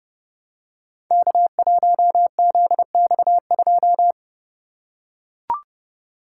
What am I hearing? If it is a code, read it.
Morse code, heard as a keyed tone, about 28 wpm, 700 Hz. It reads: K1ZX2
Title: Code Groups - 5 Letter:Numbers - 30wpm